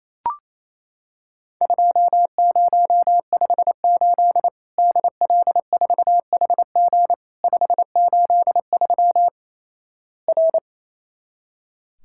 Morse code, heard as a keyed tone, about 28 wpm, 700 Hz. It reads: 2058 DL4HG 583 R
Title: Worked All Europe DX Contest - QTC Exchange - 28wpm